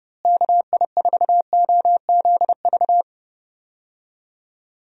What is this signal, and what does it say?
Morse code, heard as a keyed tone, about 30 wpm, 700 Hz. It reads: KI4OZV
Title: USA Call Signs - 6 Characters - Repeated 3x - 30wpm